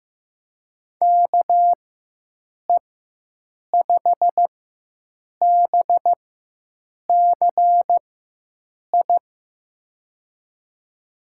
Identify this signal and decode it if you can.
Morse code, heard as a keyed tone, about 15 wpm, spaced as for 7 wpm, 700 Hz. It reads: KE5BCI
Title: USA Call Signs - 6 Characters - 4x Character Spacing - 15wpm